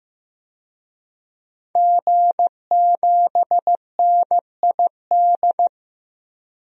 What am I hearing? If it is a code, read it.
Morse code, heard as a keyed tone, about 15 wpm, 700 Hz. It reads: G7NID
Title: DXCC-100 Call Signs - 5 Characters - Repeated 3x - 15wpm